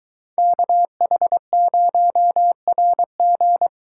Morse code, heard as a keyed tone, about 22 wpm, 700 Hz. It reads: KH0RG